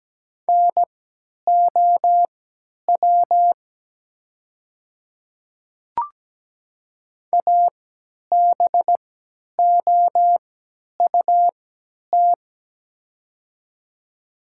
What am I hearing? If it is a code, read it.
Morse code, heard as a keyed tone, about 17 wpm, spaced as for 10 wpm, 700 Hz. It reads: NOW ABOUT
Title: 3x Character Spacing - Top 100 Words - 17wpm